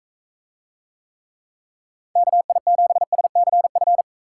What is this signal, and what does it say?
Morse code, heard as a keyed tone, about 38 wpm, 700 Hz. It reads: KI7SCF